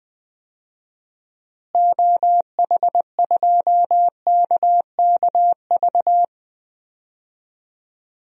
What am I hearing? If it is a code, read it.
Morse code, heard as a keyed tone, about 20 wpm, 700 Hz. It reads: OH2KKV
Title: DXCC-100 Call Signs - 6 Characters - Repeated 3x - 20wpm